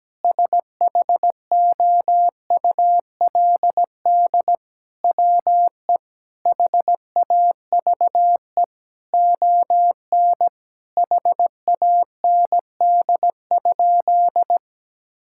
Morse code, heard as a keyed tone, about 17 wpm, 700 Hz. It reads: SHOULD WE HAVE ON HAND?
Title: Sentences from Top 1000 Words - Review Part 5 - 17wpm